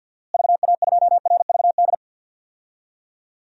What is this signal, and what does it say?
Morse code, heard as a keyed tone, about 47 wpm, 700 Hz. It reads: VA2RVD
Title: DXCC-100 Call Signs - 6 Characters - Repeated 2x - 50wpm